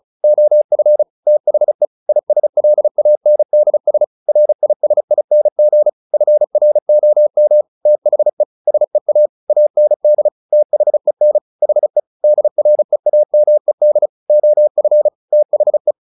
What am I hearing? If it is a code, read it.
Morse code, heard as a keyed tone, about 34 wpm, 600 Hz. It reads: OF THE ISLANDS RISING FROM THE SEA AND THEN HE DREAMED OF THE
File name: the old man and the sea / the old man and the sea 1 35wpm